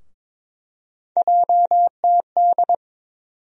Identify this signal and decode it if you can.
Morse code, heard as a keyed tone, about 22 wpm, 700 Hz. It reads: JTD
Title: USA Call Sign Postfixes - 3 Letters 22wpm